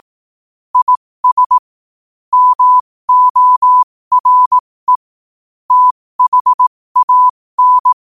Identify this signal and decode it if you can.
Morse code, heard as a keyed tone, about 18 wpm, spaced as for 15 wpm, 1000 Hz. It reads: IS MORE THAN
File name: qotdfiles / 15wpm